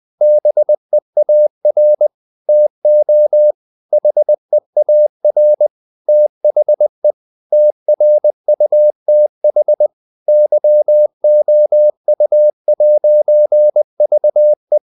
Morse code, heard as a keyed tone, about 20 wpm, 600 Hz. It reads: BEAR TO HEAR THE TRUTH YOU'VE
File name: CW / Practice / If20